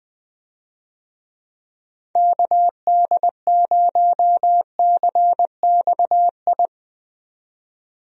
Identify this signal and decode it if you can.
Morse code, heard as a keyed tone, about 20 wpm, 700 Hz. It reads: KD0CXI
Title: USA Call Signs - 6 Characters - 20wpm